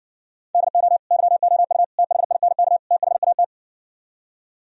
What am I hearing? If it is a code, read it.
Morse code, heard as a keyed tone, about 68 wpm, 700 Hz. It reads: DO YOU THINK THAT